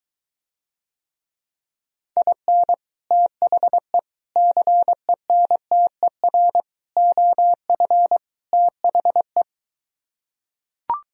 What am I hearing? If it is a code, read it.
Morse code, heard as a keyed tone, about 23 wpm, 700 Hz. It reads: IN THE CENTER OF THE